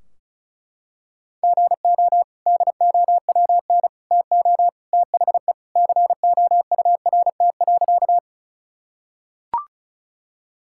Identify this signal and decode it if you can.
Morse code, heard as a keyed tone, about 36 wpm, 700 Hz. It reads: GO DOWN TO THE COURT.
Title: Sentences from Top 1600 Words 35wpm